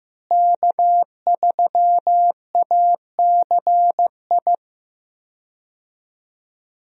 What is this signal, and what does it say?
Morse code, heard as a keyed tone, about 15 wpm, 700 Hz. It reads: K3ACI
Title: USA Call Signs - 5 Characters - Repeated 2x - 15wpm